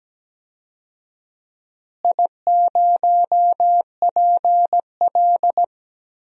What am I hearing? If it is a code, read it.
Morse code, heard as a keyed tone, about 17 wpm, 700 Hz. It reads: I0PL
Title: DXCC-100 Call Signs - 4 Characters - Repeated 3x - 17wpm